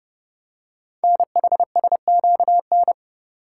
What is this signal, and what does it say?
Morse code, heard as a keyed tone, about 29 wpm, 700 Hz. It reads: NHSQN
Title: Code Groups - 5 Letter:Numbers - 30wpm